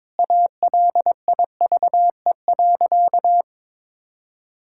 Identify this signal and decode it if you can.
Morse code, heard as a keyed tone, about 22 wpm, 700 Hz. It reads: ALIVE.
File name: practice / 22wpm / Sentences with 2-Word Phrases - 22wpm